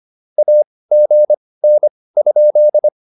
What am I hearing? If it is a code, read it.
Morse code, heard as a keyed tone, about 25 wpm, spaced as for 18 wpm, 600 Hz. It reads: AGN?
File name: QSO106 18